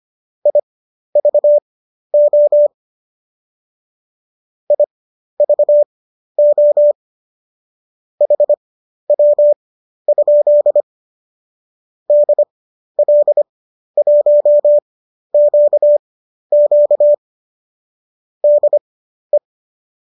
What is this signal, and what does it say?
Morse code, heard as a keyed tone, about 25 wpm, spaced as for 12 wpm, 590 Hz. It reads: IVO IVO HW? DL1QQ DE
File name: qso012 12